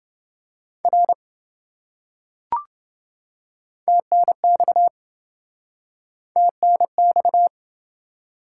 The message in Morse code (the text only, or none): R TNX TNX